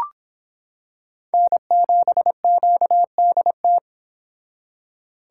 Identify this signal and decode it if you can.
Morse code, heard as a keyed tone, about 25 wpm, 700 Hz. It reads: N7QDT